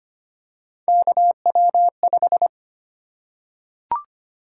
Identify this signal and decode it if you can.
Morse code, heard as a keyed tone, about 25 wpm, 700 Hz. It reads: KW5
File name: Three Letter-Number Combinations - 25wpm